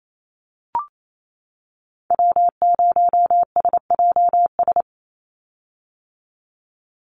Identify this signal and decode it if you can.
Morse code, heard as a keyed tone, about 28 wpm, 700 Hz. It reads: W0SJS
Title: Call Signs - 28wpm